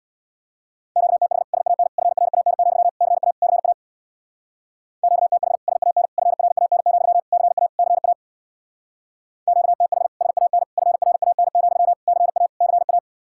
Morse code, heard as a keyed tone, about 64 wpm, 700 Hz. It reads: QTH SAN FRAN, CA CA QTH SAN FRAN, CA CA QTH SAN FRAN, CA CA